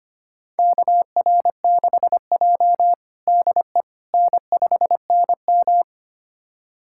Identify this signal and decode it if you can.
Morse code, heard as a keyed tone, about 25 wpm, 700 Hz. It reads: KR6J DE N5NM
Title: USA Callsign DE Callsign - 4 Characters - 25wpm